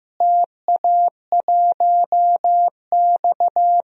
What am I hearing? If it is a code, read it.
Morse code, heard as a keyed tone, about 15 wpm, 700 Hz. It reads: TA1X
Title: DXCC-100 Call Signs - 4 Characters - Repeated 3x - 15wpm